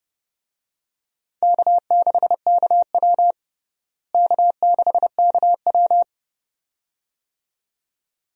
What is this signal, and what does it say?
Morse code, heard as a keyed tone, about 29 wpm, 700 Hz. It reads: K6KW K6KW
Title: USA Call Signs - 4 Characters - Repeated 2x - 30wpm